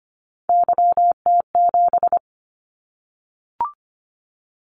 Morse code, heard as a keyed tone, about 25 wpm, 700 Hz. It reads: YT7
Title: Callsign Prefixes for Top 25 DXCC - 25wpm